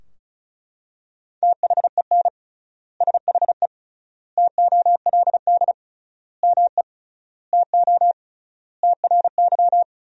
Morse code, heard as a keyed tone, about 36 wpm, 700 Hz. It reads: THEN SHE TOLD ME TO TRY
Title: 3x Word-Spacing - Sentences from Top 400 Words - 35wpm